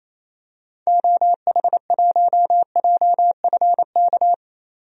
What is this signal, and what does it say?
Morse code, heard as a keyed tone, about 28 wpm, 700 Hz. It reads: OH1JFK
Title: DXCC-100 Call Signs - 6 Characters - 28wpm